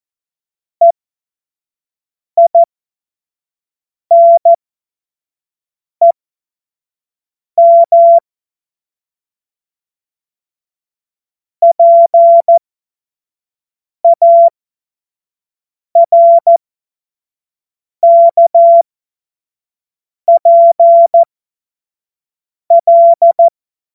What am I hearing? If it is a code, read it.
Morse code, heard as a keyed tone, about 14 wpm, spaced as for 5 wpm, 680 Hz. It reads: EINEM PARKPL
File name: Morsekurs / Witze / Witz 07. März 2023 25 F70